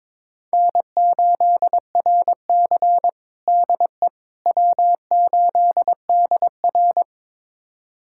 Morse code, heard as a keyed tone, about 21 wpm, 700 Hz. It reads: N8RC DE W8DR
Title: USA Callsign DE Callsign - 4 Characters - 22wpm